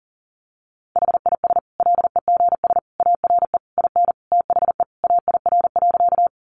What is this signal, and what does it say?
Morse code, heard as a keyed tone, about 40 wpm, 700 Hz. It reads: HIS LEGS ARE IN THE AIR.
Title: Additional Easy Phrases 40wpm